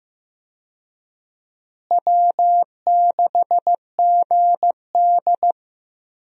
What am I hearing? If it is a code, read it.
Morse code, heard as a keyed tone, about 15 wpm, 700 Hz. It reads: W6GD
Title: USA Call Signs - 4 Characters - Repeated 2x - 15wpm